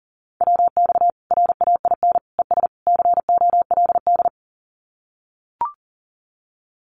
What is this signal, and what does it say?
Morse code, heard as a keyed tone, about 40 wpm, 700 Hz. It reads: WX RAIN ES COLD
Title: Common QSO Phrases 40wpm